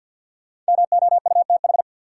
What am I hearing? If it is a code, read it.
Morse code, heard as a keyed tone, about 50 wpm, 700 Hz. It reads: MOUTH
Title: Sentences from Top 1500 Words - New Word Drill - 50wpm